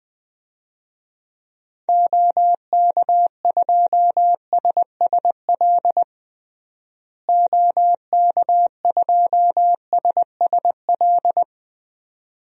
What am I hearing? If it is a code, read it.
Morse code, heard as a keyed tone, about 20 wpm, 700 Hz. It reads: OK2SSL OK2SSL
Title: DXCC-100 Call Signs - 6 Characters - Repeated 2x - 20wpm